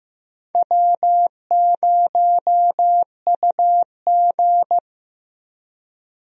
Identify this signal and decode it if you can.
Morse code, heard as a keyed tone, about 15 wpm, 700 Hz. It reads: W0UG
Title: USA Call Signs - 4 Characters - Repeated 3x - 15wpm